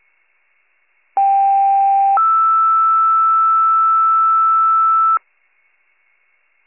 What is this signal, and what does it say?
Two-tone page, heard as a one-second tone, then a three-second tone, about 790 then 1320 Hz. It is heard from 1.2 to 5.2 s.
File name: pager / 21 RESCUE / wcalert